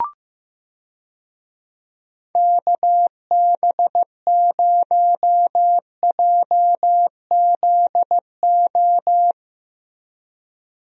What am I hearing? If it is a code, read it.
Morse code, heard as a keyed tone, about 15 wpm, 700 Hz. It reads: KB0JZO
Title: USA Call Signs - 6 Characters - Repeated 2x - 15wpm